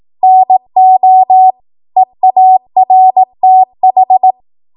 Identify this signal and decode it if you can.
Morse code, heard as a keyed tone, about 18 wpm, 750 Hz. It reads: NO EARTH
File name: Morse / 18WPM